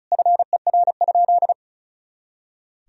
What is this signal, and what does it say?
Morse code, heard as a keyed tone, about 36 wpm, 700 Hz. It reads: FER?
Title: Sentences from Top 400 Words and Abbreviations 35wpm